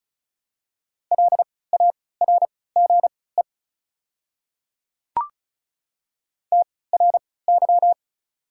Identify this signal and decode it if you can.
Morse code, heard as a keyed tone, about 36 wpm, spaced as for 20 wpm, 690 Hz. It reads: LARGE TRY
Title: 3x Character Spacing - Top 200 Words - 35wpm